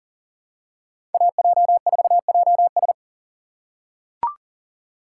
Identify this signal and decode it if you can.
Morse code, heard as a keyed tone, about 40 wpm, 700 Hz. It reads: AJ4JS
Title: USA Call Signs - 5 Characters - 6x Character Spacing - 40wpm